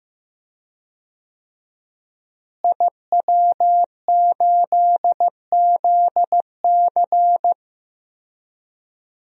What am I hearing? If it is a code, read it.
Morse code, heard as a keyed tone, about 15 wpm, 700 Hz. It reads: IW8ZC
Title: DXCC-100 Call Signs - 5 Characters - Repeated 3x - 15wpm